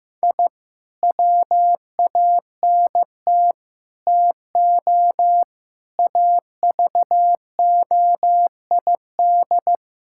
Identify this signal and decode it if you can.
Morse code, heard as a keyed tone, about 16 wpm, 700 Hz. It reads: I WANT TO AVOID
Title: Sentences from Top 1000 Words - Review Part 6 - 15wpm